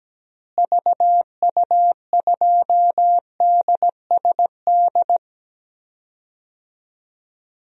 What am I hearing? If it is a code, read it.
Morse code, heard as a keyed tone, about 17 wpm, 700 Hz. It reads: VU2DSD
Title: DXCC-100 Call Signs - 6 Characters - Repeated 2x - 17wpm